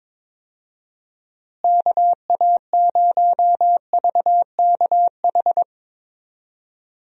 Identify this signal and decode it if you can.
Morse code, heard as a keyed tone, about 22 wpm, 700 Hz. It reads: KA0VKH
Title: USA Call Signs - 6 Characters - Repeated 3x - 22wpm